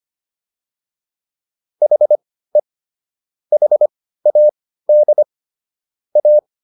Morse code, heard as a keyed tone, about 25 wpm, spaced as for 15 wpm, 600 Hz. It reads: HE HAD A